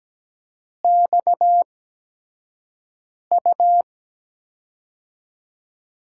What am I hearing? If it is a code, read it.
Morse code, heard as a keyed tone, about 17 wpm, spaced as for 5 wpm, 700 Hz. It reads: XU
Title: USA Call Signs - 6 Characters - 8x Character Spacing - 17wpm